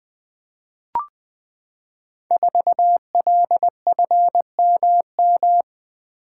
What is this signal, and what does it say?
Morse code, heard as a keyed tone, about 20 wpm, 700 Hz. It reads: 4LFMM
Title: Code Groups - 5 Letter:Numbers - 20wpm